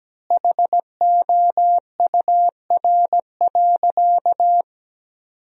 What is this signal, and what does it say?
Morse code, heard as a keyed tone, about 17 wpm, 700 Hz. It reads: HOUR.
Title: 4x Word-Spacing - Sentences from Top 500 Words - 17wpm